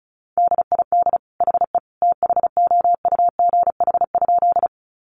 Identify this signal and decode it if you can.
Morse code, heard as a keyed tone, about 34 wpm, 700 Hz. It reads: DID HE THOUGH?
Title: Sentences from Top 400 Words 35wpm